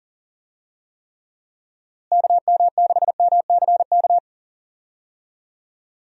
Morse code, heard as a keyed tone, about 40 wpm, 700 Hz. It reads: KM6MCK
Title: USA Call Signs - 6 Characters - Repeated 2x - 40wpm